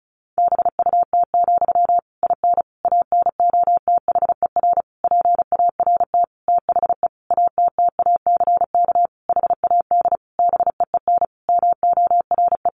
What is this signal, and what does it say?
Morse code, heard as a keyed tone, about 34 wpm, 700 Hz. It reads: BUT, IN ANOTHER PART THE ATTACK HAD BEEN MORE
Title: Sentences from Top 1500 Words 35wpm